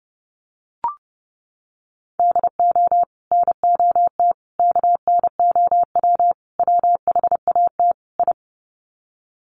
Morse code, heard as a keyed tone, about 30 wpm, 700 Hz. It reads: DO NOT KNOW WHAT I
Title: Sets of 5 Words 30wpm